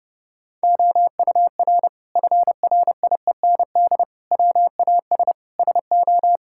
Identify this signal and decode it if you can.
Morse code, heard as a keyed tone, about 32 wpm, 700 Hz. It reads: OUR FRIEND WAS SO
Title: Sentences from Top 1000 Words - Review Part 5 - 30wpm